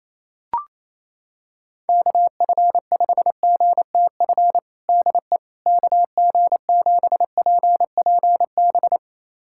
Morse code, heard as a keyed tone, about 28 wpm, 700 Hz. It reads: KF5GTF DE KG7PPB
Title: USA Callsign DE Callsign 28wpm